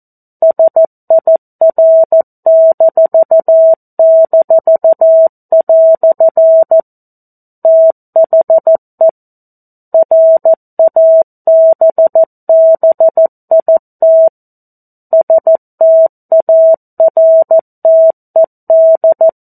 Morse code, heard as a keyed tone, about 14 wpm, 650 Hz. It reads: SIR--" THE RABBIT STARTED
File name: 02-Chapter-02 14 WPM